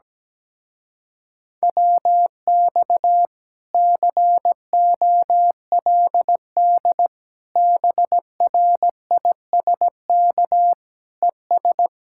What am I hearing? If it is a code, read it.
Morse code, heard as a keyed tone, about 17 wpm, 700 Hz. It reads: WX COLD BRISK ES
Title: Speed-Racing Common QSO Phrases - 17wpm